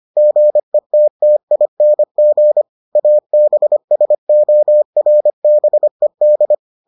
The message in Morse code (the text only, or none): GETTING ABSORBED